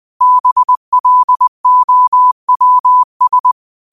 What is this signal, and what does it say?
Morse code, heard as a keyed tone, about 20 wpm, 1000 Hz. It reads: BLOWS
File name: qotdfiles / 20wpm